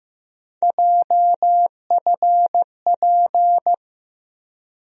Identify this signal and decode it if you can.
Morse code, heard as a keyed tone, about 15 wpm, 700 Hz. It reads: JFP
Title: USA Call Sign Postfixes - 3 Letters 15wpm